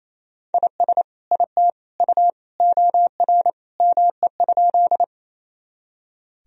Morse code, heard as a keyed tone, about 27 wpm, 700 Hz. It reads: IS IT U OR ME?
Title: Sentences from Top 200 Words and Abbreviations 28wpm